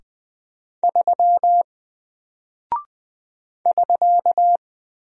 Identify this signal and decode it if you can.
Morse code, heard as a keyed tone, about 20 wpm, 700 Hz. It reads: 3 <SK>
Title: Single Character (Letter:Number:Punctuation:Prosign) 20wpm